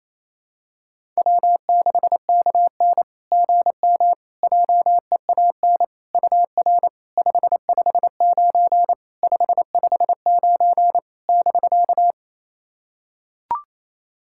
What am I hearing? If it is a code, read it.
Morse code, heard as a keyed tone, about 28 wpm, 690 Hz. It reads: W6KN GM JEAN UR 559 559 <BK>
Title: SOTA QSO Practice 28wpm